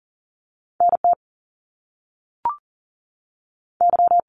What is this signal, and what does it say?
Morse code, heard as a keyed tone, about 40 wpm, 700 Hz. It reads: NT Y
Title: Canadian Provinces and Territories 40wpm